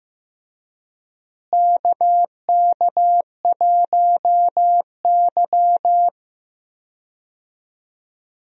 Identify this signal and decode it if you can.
Morse code, heard as a keyed tone, about 15 wpm, 700 Hz. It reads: KK1Y
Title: USA Call Signs - 4 Characters - Repeated 2x - 15wpm